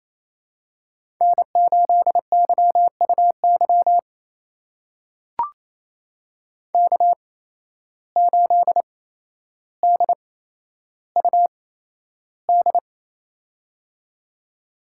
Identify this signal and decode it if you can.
Morse code, heard as a keyed tone, about 28 wpm, spaced as for 12 wpm, 700 Hz. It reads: N8YUY K8DUD
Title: USA Call Signs - 5 Characters - 8x Character Spacing - 28wpm